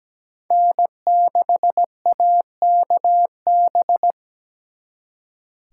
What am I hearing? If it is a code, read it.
Morse code, heard as a keyed tone, about 20 wpm, 700 Hz. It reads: N6AKB
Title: Speed Racing - USA 1x3 Call Signs 17wpm